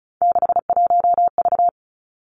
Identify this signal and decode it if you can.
Morse code, heard as a keyed tone, about 36 wpm, 700 Hz. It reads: 61V
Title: Three Challenging Letter-Number Combinations 35wpm